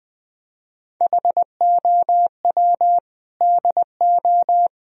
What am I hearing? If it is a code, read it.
Morse code, heard as a keyed tone, about 20 wpm, 700 Hz. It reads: HOW DO
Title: Sentences from Top 1000 Words - Review Part 3 - 20wpm